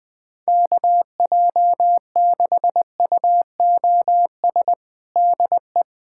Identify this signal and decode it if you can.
Morse code, heard as a keyed tone, about 20 wpm, 700 Hz. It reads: KJ6UOS DE
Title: USA Callsign DE Callsign - 6 Characters - 20wpm